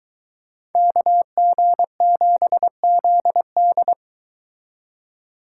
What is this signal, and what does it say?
Morse code, heard as a keyed tone, about 23 wpm, 700 Hz. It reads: KG7ZD